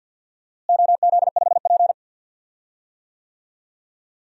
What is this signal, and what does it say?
Morse code, heard as a keyed tone, about 53 wpm, 700 Hz. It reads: OZ5P
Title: DXCC-100 Call Signs - 4 Characters - Repeated 2x - 50wpm